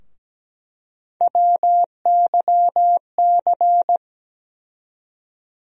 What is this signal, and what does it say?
Morse code, heard as a keyed tone, about 17 wpm, 700 Hz. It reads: WYC
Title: USA Call Sign Postfixes - 3 Letters 17wpm